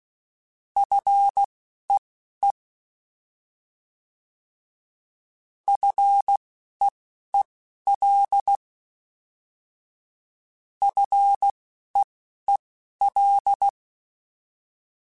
Morse code, heard as a keyed tone, about 16 wpm, spaced as for 11 wpm, 800 Hz. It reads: FEE FEEL FEEL
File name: Short Runs / Short Runs-mp3s / SinglePrefixes @10